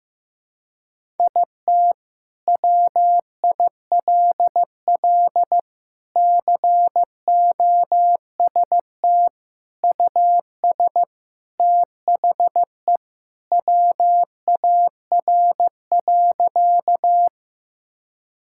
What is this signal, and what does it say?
Morse code, heard as a keyed tone, about 15 wpm, 700 Hz. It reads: IT WILL COST US THE WAR.
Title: Sentences from Top 1000 Words - Review Part 5 - 15wpm